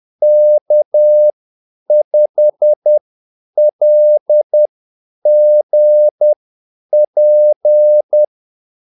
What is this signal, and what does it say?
Morse code, heard as a keyed tone, about 10 wpm, spaced as for 8 wpm, 600 Hz. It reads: K5LGP
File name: Callsigns-10-8wpm